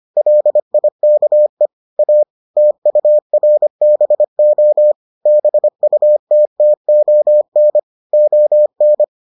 Morse code, heard as a keyed tone, about 25 wpm, 600 Hz. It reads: LIKE A TURBO BUTTON ON